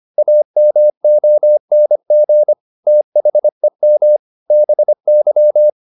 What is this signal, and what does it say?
Morse code, heard as a keyed tone, about 25 wpm, 600 Hz. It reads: AMONG THEM BY
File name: the old man and the sea part 1 25wpm